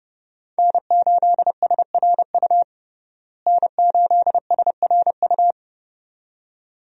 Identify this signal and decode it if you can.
Morse code, heard as a keyed tone, about 30 wpm, 700 Hz. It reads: N8SRU N8SRU
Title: USA Call Signs - 5 Characters - Repeated 2x - 30wpm